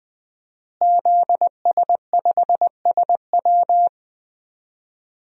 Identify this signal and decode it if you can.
Morse code, heard as a keyed tone, about 20 wpm, 700 Hz. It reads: ZS5SW
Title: DXCC-100 Call Signs - 5 Characters - Repeated 2x - 20wpm